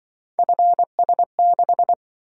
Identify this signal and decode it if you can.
Morse code, heard as a keyed tone, about 24 wpm, 700 Hz. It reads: FS6
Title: Speed-Racing - Three Letter-Number Combinations - 20wpm